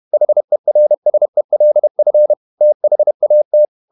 Morse code, heard as a keyed tone, about 31 wpm, 600 Hz. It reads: HERSELF THAT
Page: Never 31wpm | CW med Gnister